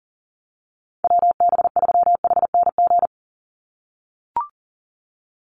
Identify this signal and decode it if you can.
Morse code, heard as a keyed tone, about 40 wpm, 700 Hz. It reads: WB3HNG
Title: USA Call Signs 40wpm